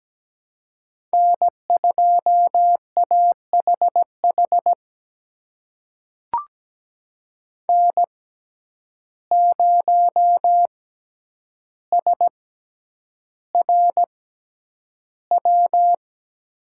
Morse code, heard as a keyed tone, about 17 wpm, spaced as for 9 wpm, 700 Hz. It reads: N2AHH N0SRW
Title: USA Call Signs - 5 Characters - 6x Character Spacing - 17wpm